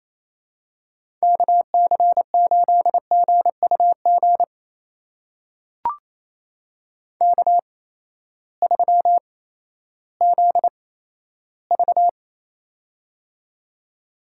Morse code, heard as a keyed tone, about 28 wpm, spaced as for 14 wpm, 700 Hz. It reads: KC8GUG K3ZV
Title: USA Call Signs - 8x Character Spacing - 28wpm